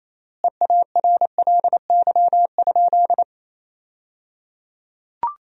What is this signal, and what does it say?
Morse code, heard as a keyed tone, about 28 wpm, 700 Hz. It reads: EARLY?
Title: 3x Word-Spacing - Sentences from Top 400 Words - 28wpm